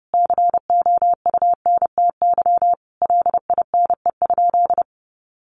Morse code, heard as a keyed tone, about 30 wpm, 700 Hz. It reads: COUNTY LINE?
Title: Sentences from Top 1200 Words 30wpm